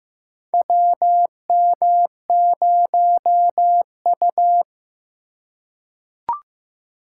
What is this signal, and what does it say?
Morse code, heard as a keyed tone, about 15 wpm, 700 Hz. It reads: WM0U
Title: USA Call Signs - 4 Characters - 4x Character Spacing - 15wpm